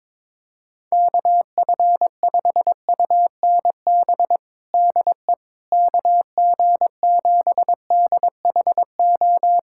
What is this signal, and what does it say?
Morse code, heard as a keyed tone, about 21 wpm, 700 Hz. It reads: KF5UNB DE KG7DHO
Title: USA Callsign DE Callsign - 6 Characters - 22wpm